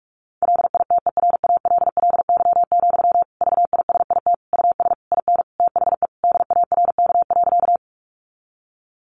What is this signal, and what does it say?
Morse code, heard as a keyed tone, about 44 wpm, 700 Hz. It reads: LITERALLY, VISIT US IN THE DARK.
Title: Sentences from Top 1400 Words 45wpm